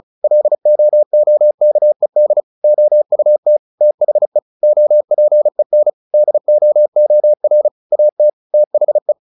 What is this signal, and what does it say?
Morse code, heard as a keyed tone, about 35 wpm, 600 Hz. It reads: LOOKED OUT THE OPEN DOOR AT THE
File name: the old man and the sea / the old man and the sea 1 35wpm